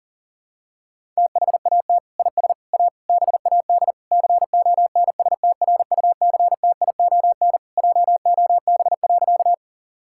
Morse code, heard as a keyed tone, about 40 wpm, 700 Hz. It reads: THAT IS A BAD CONSTRUCTION JOB.
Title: Sentences from Top 1600 Words 40wpm